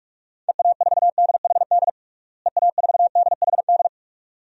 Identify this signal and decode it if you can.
Morse code, heard as a keyed tone, about 45 wpm, 700 Hz. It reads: EA4DHD EA4DHD
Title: DXCC-100 Call Signs - 6 Characters - Repeated 2x - 45wpm